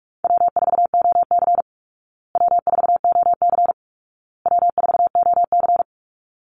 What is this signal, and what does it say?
Morse code, heard as a keyed tone, about 44 wpm, 700 Hz. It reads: W4OC W4OC W4OC
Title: Easy Call Signs Repeated 3x - 45wpm